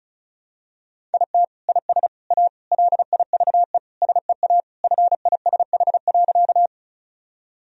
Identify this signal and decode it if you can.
Morse code, heard as a keyed tone, about 36 wpm, 700 Hz. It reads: IT IS A LIVE SEA FISH.
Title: Sentences from Top 1000 Words - Review Part 1 - 35wpm